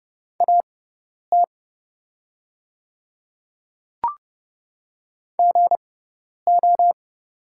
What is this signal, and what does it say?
Morse code, heard as a keyed tone, about 31 wpm, spaced as for 10 wpm, 700 Hz. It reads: AT GO
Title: 6x Character Spacing - Top 300 Words - 30wpm